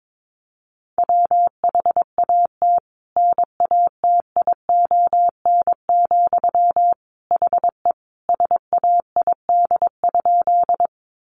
Morse code, heard as a keyed tone, about 22 wpm, 700 Hz. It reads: WHAT NATION, HE SAID?
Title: Sentences from Top 900 Words 22wpm